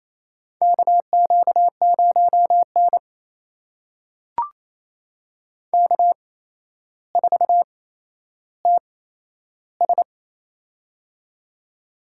Morse code, heard as a keyed tone, about 28 wpm, spaced as for 12 wpm, 700 Hz. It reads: KQ0N K4TS
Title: USA Call Signs - 4 Characters - 8x Character Spacing - 28wpm